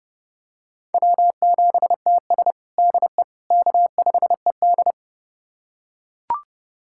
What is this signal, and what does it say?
Morse code, heard as a keyed tone, about 30 wpm, 700 Hz. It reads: W7TS DE K5ED
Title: USA Callsign DE Callsign - 4 Characters - 30wpm